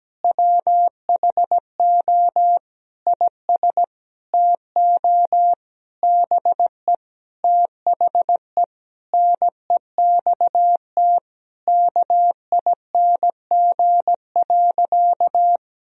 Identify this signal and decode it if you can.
Morse code, heard as a keyed tone, about 17 wpm, 700 Hz. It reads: WHO IS TO BE THE NEXT KING.
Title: Sentences from Top 1000 Words - Review Part 3 - 17wpm